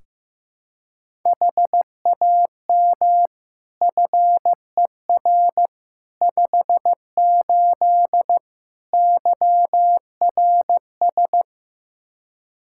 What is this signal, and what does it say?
Morse code, heard as a keyed tone, about 16 wpm, 700 Hz. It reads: HAM FER 58 YRS
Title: Practice QSO - Exchange 3 - More Info - 15wpm